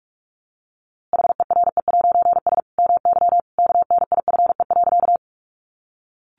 Morse code, heard as a keyed tone, about 46 wpm, 700 Hz. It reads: HERE'S MY KNIFE.
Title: A Sound Of Thunder 45wpm